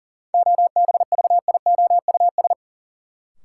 Morse code, heard as a keyed tone, about 40 wpm, 700 Hz. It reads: OBVIOUS
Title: Sentences from Top 1600 Words - New Word Drill - 40wpm